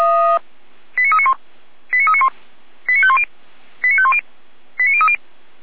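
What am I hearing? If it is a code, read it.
Five-tone selective call (ZVEI), heard as quick runs of five tones, tones of about 70 ms. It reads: 89391 89391 89510 89510 89030
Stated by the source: ZVEI five-tone selective call received on SDR.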